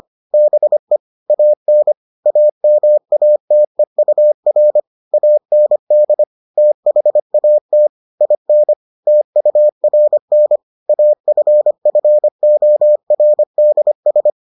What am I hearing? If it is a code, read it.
Morse code, heard as a keyed tone, about 25 wpm, 600 Hz. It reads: BE AN AMATEUR AND THAT IN TURN AFFORDS